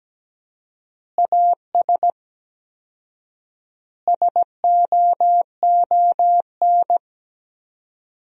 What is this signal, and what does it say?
Morse code, heard as a keyed tone, about 17 wpm, 700 Hz. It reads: AS SOON
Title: 4x Word-Spacing - Sentences from Top 500 Words - 17wpm